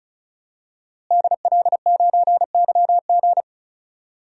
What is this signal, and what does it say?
Morse code, heard as a keyed tone, about 36 wpm, 700 Hz. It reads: DL9YG
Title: DXCC-100 Call Signs - 5 Characters - Repeated 2x - 35wpm